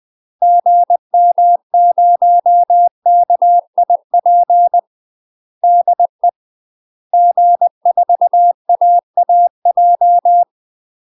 Morse code, heard as a keyed tone, about 20 wpm, 700 Hz. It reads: GM0KIP DE G4AAJ
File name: UK MASTER QSOs TRAINING / 20WPM / UK MASTER QSOs 0074